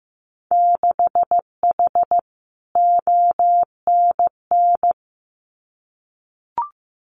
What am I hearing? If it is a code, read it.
Morse code, heard as a keyed tone, about 15 wpm, 700 Hz. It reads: 6H ONN
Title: Winter Field Day Exchanges Encore 15wpm